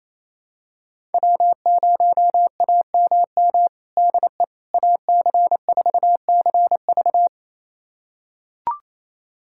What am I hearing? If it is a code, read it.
Morse code, heard as a keyed tone, about 28 wpm, 700 Hz. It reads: W0AMM DE AC4CV
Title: USA Callsign DE Callsign 28wpm